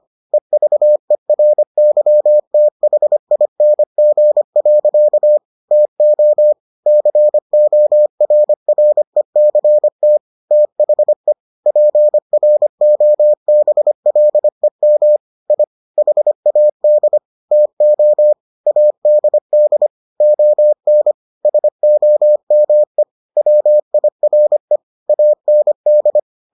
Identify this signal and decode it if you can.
Morse code, heard as a keyed tone, about 25 wpm, 600 Hz. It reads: EVERYTHING. TO CORRECT THE PROBLEM I HAD TO ADD ON SOME WIRE AND